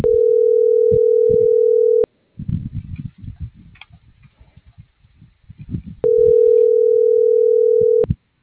Ringback tone (North America), 440 + 480 Hz.